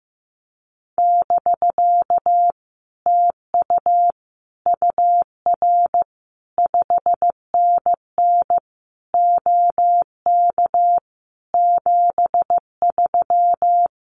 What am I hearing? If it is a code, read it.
Morse code, heard as a keyed tone, about 15 wpm, 700 Hz. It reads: <BK> TU UR 5NN OK 73
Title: Common QSO Phrases Encore 15wpm